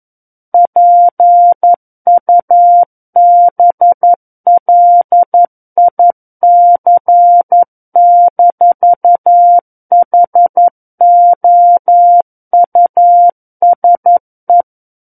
Morse code, heard as a keyed tone, about 11 wpm, 700 Hz. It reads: PUBLIC-HOUSE
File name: Book One / 02-Chapter 2 - 11 WPM